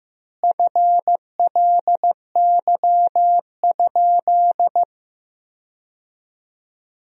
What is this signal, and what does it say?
Morse code, heard as a keyed tone, about 15 wpm, 700 Hz. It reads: FLY?
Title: 4x Word-Spacing - Sentences from Top 500 Words - 15wpm